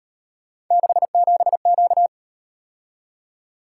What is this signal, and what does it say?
Morse code, heard as a keyed tone, about 37 wpm, 700 Hz. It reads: 67Q